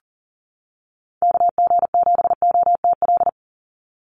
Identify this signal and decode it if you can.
Morse code, heard as a keyed tone, about 40 wpm, 700 Hz. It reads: KG7OTL
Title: USA Call Signs 40wpm